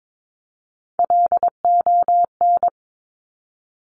Morse code, heard as a keyed tone, about 22 wpm, 700 Hz. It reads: LON
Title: Top Three Letter Combinations in Words 22wpm